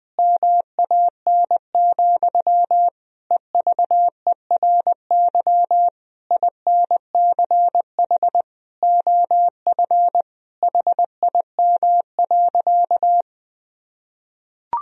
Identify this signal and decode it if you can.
Morse code, heard as a keyed tone, about 20 wpm, 700 Hz. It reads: MAN, EVERY INCH OF HIM.
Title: Sentences from Top 1000 Words - Review Part 3 - 20wpm